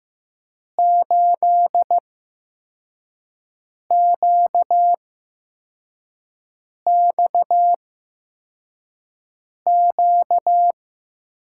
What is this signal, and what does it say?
Morse code, heard as a keyed tone, about 15 wpm, spaced as for 4 wpm, 700 Hz. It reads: 8QXQ
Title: USA Call Signs - 6 Characters - 8x Character Spacing - 15wpm